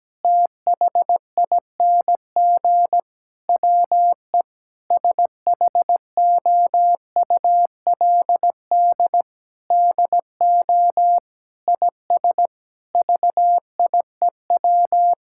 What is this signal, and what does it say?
Morse code, heard as a keyed tone, about 17 wpm, 700 Hz. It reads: THING WE SHOULD DO IS VIEW
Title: Sentences from Top 1000 Words - Review Part 6 - 17wpm